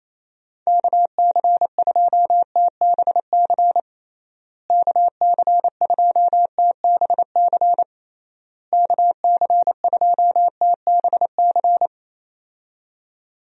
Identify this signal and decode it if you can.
Morse code, heard as a keyed tone, about 28 wpm, 700 Hz. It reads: KC2TBC KC2TBC KC2TBC
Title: USA Call Signs - 6 Characters - Repeated 3x - 28wpm